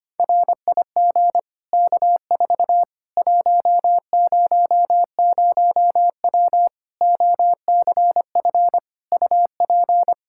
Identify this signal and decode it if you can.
Morse code, heard as a keyed tone, about 25 wpm, 700 Hz. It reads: RIG K4 100W OCF UP
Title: Practice QSO - Exchange 2 - Info - 25wpm